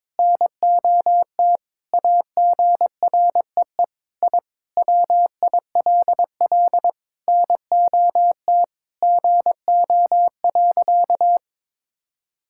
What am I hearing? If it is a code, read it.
Morse code, heard as a keyed tone, about 22 wpm, 700 Hz. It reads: NOT AGREE I WILL NOT GO.
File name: practice / 22wpm / Sentences with 4-Word Phrases - 22wpm